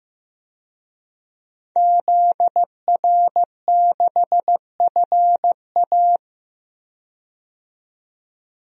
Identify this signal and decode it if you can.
Morse code, heard as a keyed tone, about 15 wpm, 700 Hz. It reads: ZR6FA
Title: DXCC-100 Call Signs - 5 Characters - 15wpm